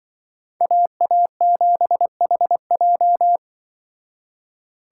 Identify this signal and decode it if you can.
Morse code, heard as a keyed tone, about 24 wpm, 700 Hz. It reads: AA7HJ
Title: Speed Racing - USA 2x2 Call Signs 20wpm